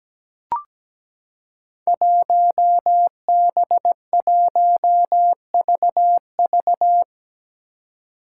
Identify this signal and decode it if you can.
Morse code, heard as a keyed tone, about 17 wpm, 700 Hz. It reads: 1B1VV
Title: Five Challenging Letter-Number Combinations 17wpm